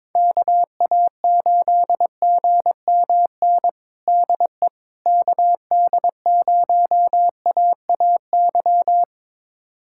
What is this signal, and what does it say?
Morse code, heard as a keyed tone, about 22 wpm, 700 Hz. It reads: KA8GMN DE KD0AAY
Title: USA Callsign DE Callsign 22wpm